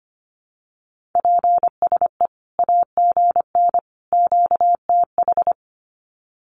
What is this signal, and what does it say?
Morse code, heard as a keyed tone, about 25 wpm, 700 Hz. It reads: PSE AGN QTH
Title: Common QSO Phrases 25wpm